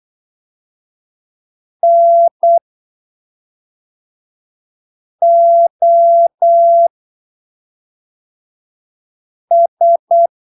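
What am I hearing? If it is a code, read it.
Morse code, heard as a keyed tone, about 8 wpm, spaced as for 3 wpm, 680 Hz. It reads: NOS
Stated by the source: Morse training text at 8 ppm for Brazilian class B amateur radio telegraphy exams, from Portal do CW, PY2BRL.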